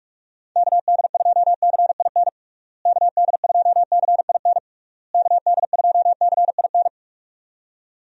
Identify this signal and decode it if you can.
Morse code, heard as a keyed tone, about 42 wpm, 700 Hz. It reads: KD2CIN KD2CIN KD2CIN
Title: USA Call Signs - 6 Characters - Repeated 3x - 45wpm